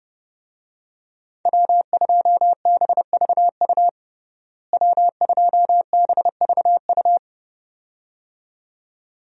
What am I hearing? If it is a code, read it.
Morse code, heard as a keyed tone, about 29 wpm, 700 Hz. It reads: W2BVU W2BVU
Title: USA Call Signs - 5 Characters - Repeated 2x - 30wpm